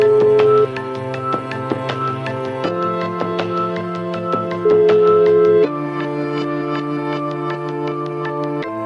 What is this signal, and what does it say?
Ringback tone (Europe, 425 Hz), with noise.